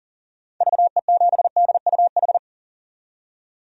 Morse code, heard as a keyed tone, about 40 wpm, 700 Hz. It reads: VE7DUH
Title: DXCC-100 Call Signs - 6 Characters - 40wpm